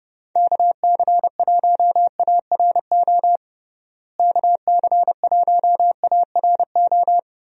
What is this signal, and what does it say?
Morse code, heard as a keyed tone, about 30 wpm, 700 Hz. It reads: KC1ARO KC1ARO
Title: USA Call Signs - 6 Characters - Repeated 3x - 30wpm